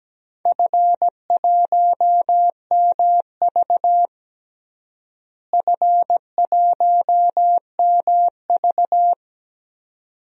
Morse code, heard as a keyed tone, about 17 wpm, 700 Hz. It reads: F1MV F1MV
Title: DXCC-100 Call Signs - 4 Characters - Repeated 2x - 17wpm